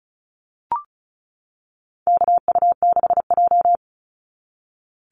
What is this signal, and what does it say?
Morse code, heard as a keyed tone, about 36 wpm, 700 Hz. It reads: KU6J
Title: NA SOTA Chaser Callsigns 35wpm